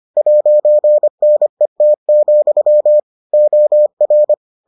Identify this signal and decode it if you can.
Morse code, heard as a keyed tone, about 25 wpm, 600 Hz. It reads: 'NET, OR